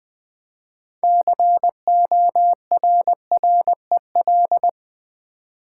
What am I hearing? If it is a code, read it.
Morse code, heard as a keyed tone, about 20 wpm, 700 Hz. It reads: CORREL
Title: Build A Word - 3 Letters At A Time - 9-Letter Words - 20wpm